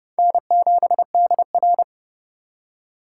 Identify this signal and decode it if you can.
Morse code, heard as a keyed tone, about 32 wpm, 700 Hz. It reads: N7DR
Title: K1USN SST QSO Practice - 30wpm